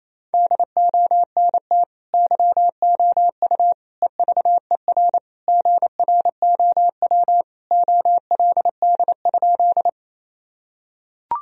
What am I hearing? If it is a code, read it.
Morse code, heard as a keyed tone, about 28 wpm, 700 Hz. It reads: DONT YOU EVER GROW OLD?
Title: Sentences from Top 1000 Words - Review Part 1 - 28wpm